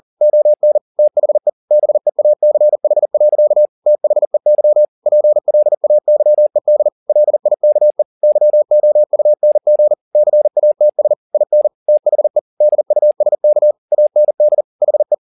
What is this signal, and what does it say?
Morse code, heard as a keyed tone, about 40 wpm, 600 Hz. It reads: ON THE BEACH. THEY PLAYED LIKE YOUNG CATS IN THE DUSK AND HE